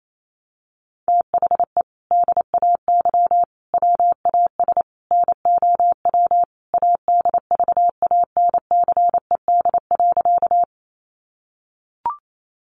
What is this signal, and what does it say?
Morse code, heard as a keyed tone, about 28 wpm, 700 Hz. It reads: THE DAY WAS NOW ADVANCED.
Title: Sentences from Top 1200 Words 28wpm